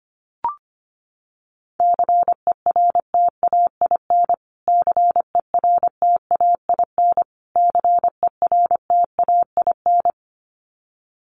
Morse code, heard as a keyed tone, about 25 wpm, 700 Hz. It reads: CERTAIN CERTAIN CERTAIN
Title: Top 500 Words Repeated 25wpm